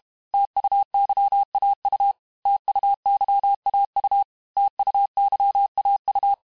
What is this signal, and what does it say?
Morse code, heard as a keyed tone, about 32 wpm, 800 Hz. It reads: TUYAU TUYAU TUYAU
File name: cours-du-mercredi / audios / cours-868-teste-MOTS-vitesse-30mpm